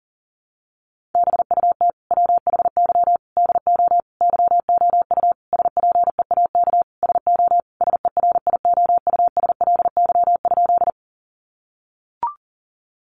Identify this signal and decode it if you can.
Morse code, heard as a keyed tone, about 40 wpm, 700 Hz. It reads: BUT WHY DO YOU SPEAK SO SERIOUSLY?
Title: Sentences from Top 1200 Words 40wpm